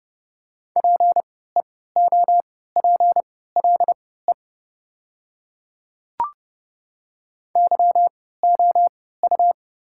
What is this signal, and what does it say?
Morse code, heard as a keyed tone, about 28 wpm, spaced as for 17 wpm, 690 Hz. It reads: PEOPLE YOU
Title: 3x Character Spacing - Top 200 Words - 30wpm